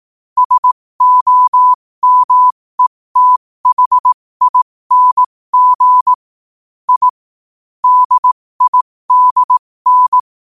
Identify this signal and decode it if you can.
Morse code, heard as a keyed tone, about 18 wpm, spaced as for 15 wpm, 1000 Hz. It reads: SOMETHING I DIDN
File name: qotdfiles / 15wpm